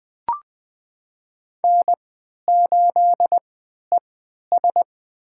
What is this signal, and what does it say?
Morse code, heard as a keyed tone, about 20 wpm, spaced as for 11 wpm, 700 Hz. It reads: N8ES
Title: USA Call Signs - 4 Characters - 3x Character Spacing - 20wpm